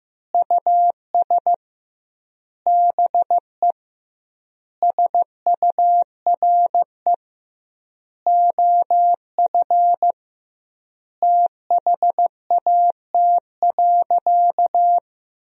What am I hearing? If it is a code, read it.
Morse code, heard as a keyed tone, about 15 wpm, 700 Hz. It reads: US BE SURE OF THAT.
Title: 2x Word-Spacing - Sentences from Top 300 Words - 15wpm